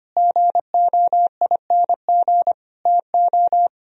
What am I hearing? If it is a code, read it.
Morse code, heard as a keyed tone, about 31 wpm, 700 Hz. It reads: GOING TO
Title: Speed-Racing - Sets of 4 Words 25wpm